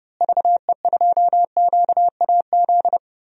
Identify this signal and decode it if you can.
Morse code, heard as a keyed tone, about 30 wpm, 700 Hz. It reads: VE2QAZ
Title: DXCC-100 Call Signs - 6 Characters - Repeated 2x - 30wpm